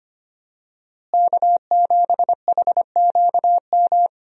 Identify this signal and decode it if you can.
Morse code, heard as a keyed tone, about 25 wpm, 700 Hz. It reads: K7HQM
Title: USA Call Signs - 5 Characters - Repeated 2x - 25wpm